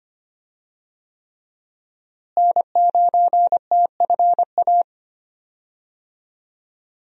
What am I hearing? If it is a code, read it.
Morse code, heard as a keyed tone, about 25 wpm, 700 Hz. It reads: N9TFA
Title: USA Call Signs - 5 Characters - Repeated 3x - 25wpm